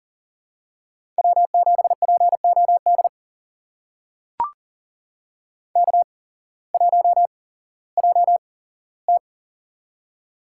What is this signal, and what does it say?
Morse code, heard as a keyed tone, about 43 wpm, spaced as for 19 wpm, 700 Hz. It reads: W7POD K1JT
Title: USA Call Signs - 8x Character Spacing - 40wpm